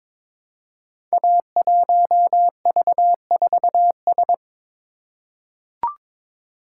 Morse code, heard as a keyed tone, about 22 wpm, 700 Hz. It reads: A1V4S
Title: Code Groups - 5 Letter:Numbers - 22wpm